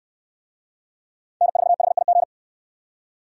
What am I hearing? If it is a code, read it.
Morse code, heard as a keyed tone, about 65 wpm, 700 Hz. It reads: N4HIG